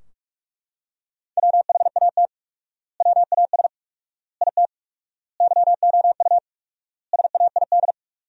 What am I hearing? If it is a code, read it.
Morse code, heard as a keyed tone, about 41 wpm, 700 Hz. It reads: WHAT WAS IT YOU SAID
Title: 4x Word-Spacing - Sentences from Top 200 Words - 45wpm